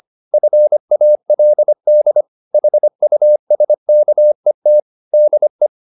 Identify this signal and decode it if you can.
Morse code, heard as a keyed tone, about 25 wpm, 600 Hz. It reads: FALD HUSKET DE
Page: Ferie 25 wpm